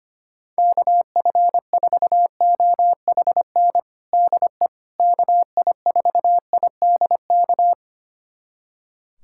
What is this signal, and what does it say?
Morse code, heard as a keyed tone, about 25 wpm, 700 Hz. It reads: KF4OHN DE KI4IDK
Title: USA Callsign DE Callsign - 6 Characters - 25wpm